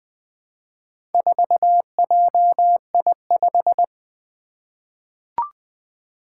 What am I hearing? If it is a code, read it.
Morse code, heard as a keyed tone, about 20 wpm, 700 Hz. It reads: 4JI5
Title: Four Challenging Letter-Number Combinations 20wpm